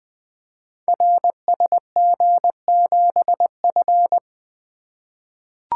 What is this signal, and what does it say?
Morse code, heard as a keyed tone, about 20 wpm, 700 Hz. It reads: RSG7F
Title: Code Groups - 5 Letter:Numbers - 20wpm